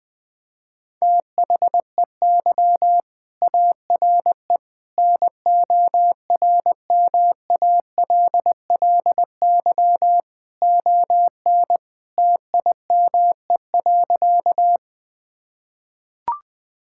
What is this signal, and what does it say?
Morse code, heard as a keyed tone, about 20 wpm, 700 Hz. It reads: THEY ARE NORMALLY ON TIME.
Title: Sentences from Top 1000 Words - Review Part 6 - 20wpm